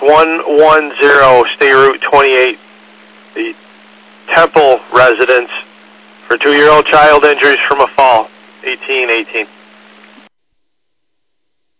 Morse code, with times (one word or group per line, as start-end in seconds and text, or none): none